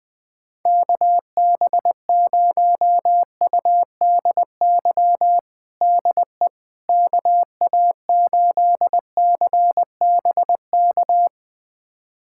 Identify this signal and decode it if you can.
Morse code, heard as a keyed tone, about 20 wpm, 700 Hz. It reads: KB0UDY DE KA8CBK
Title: USA Callsign DE Callsign - 6 Characters - 20wpm